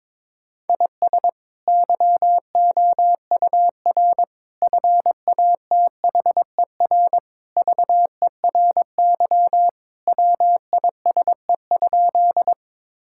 Morse code, heard as a keyed tone, about 22 wpm, 700 Hz. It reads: IS YOUR FATHER VERY WISE?
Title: Sentences from Top 1600 Words 22wpm